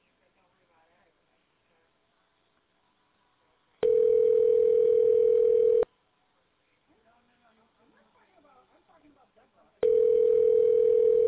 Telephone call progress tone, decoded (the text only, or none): ringback tone